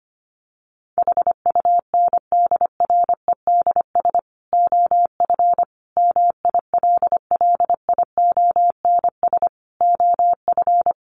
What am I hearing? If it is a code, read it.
Morse code, heard as a keyed tone, about 25 wpm, 700 Hz. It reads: HUNDREDS OF MILLIONS OF
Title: Sets of 5 Words 25wpm